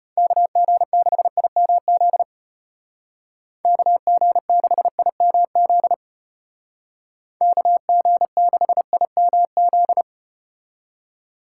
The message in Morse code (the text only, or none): KG6IMZ KG6IMZ KG6IMZ